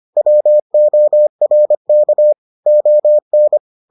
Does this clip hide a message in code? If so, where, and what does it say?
Yes: Morse code from 0.2 to 3.6 s, heard as a keyed tone, about 25 wpm, 600 Hz. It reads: WORK ON